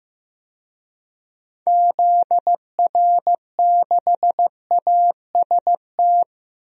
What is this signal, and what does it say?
Morse code, heard as a keyed tone, about 15 wpm, 700 Hz. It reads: ZR6AST
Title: DXCC-100 Call Signs - 6 Characters - 15wpm